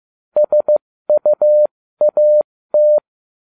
Morse code, heard as a keyed tone, about 15 wpm, spaced as for 13 wpm, 600 Hz. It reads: SUAT